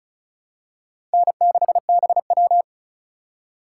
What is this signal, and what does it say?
Morse code, heard as a keyed tone, about 36 wpm, 700 Hz. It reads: N6BW
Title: USA Call Signs - 4 Characters - 2x Character Spacing - 35wpm